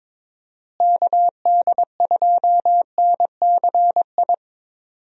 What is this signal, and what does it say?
Morse code, heard as a keyed tone, about 22 wpm, 700 Hz. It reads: KD2NCI
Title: USA Call Signs - 6 Characters - Repeated 2x - 22wpm